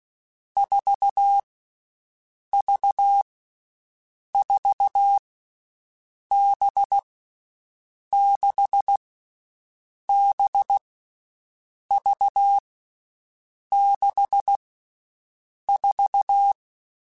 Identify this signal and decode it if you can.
Morse code, heard as a keyed tone, about 16 wpm, 800 Hz. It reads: 4 V 4 B 6 B V 6 4
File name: Groups@10 / Groups@10 mp3s / VB46@10